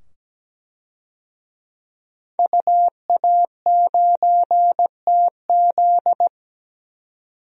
Morse code, heard as a keyed tone, about 17 wpm, 700 Hz. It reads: UA9TZ
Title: DXCC-100 Call Signs - 5 Characters - 17wpm